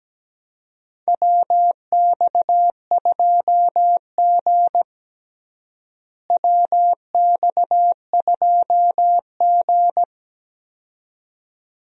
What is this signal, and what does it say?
Morse code, heard as a keyed tone, about 17 wpm, 700 Hz. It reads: WX2G WX2G
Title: USA Call Signs - 4 Characters - Repeated 2x - 17wpm